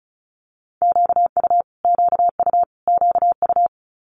Morse code, heard as a keyed tone, about 36 wpm, 700 Hz. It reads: QU QU QU
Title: Multi-letter Phonemes Repeated 3x 35wpm